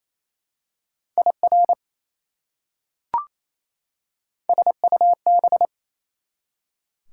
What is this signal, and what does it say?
Morse code, heard as a keyed tone, about 28 wpm, 700 Hz. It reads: IR SUB
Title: Prefixes and Suffixes 28wpm